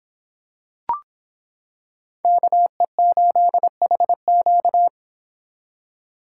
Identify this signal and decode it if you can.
Morse code, heard as a keyed tone, about 26 wpm, 700 Hz. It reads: KE8HQ